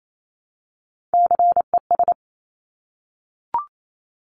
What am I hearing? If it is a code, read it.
Morse code, heard as a keyed tone, about 28 wpm, 700 Hz. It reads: CES
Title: Top Three Letter Combinations in Words 28wpm